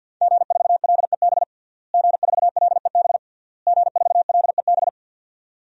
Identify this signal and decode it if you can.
Morse code, heard as a keyed tone, about 48 wpm, 700 Hz. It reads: G4LEB G4LEB G4LEB
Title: DXCC-100 Call Signs - 5 Characters - Repeated 3x - 50wpm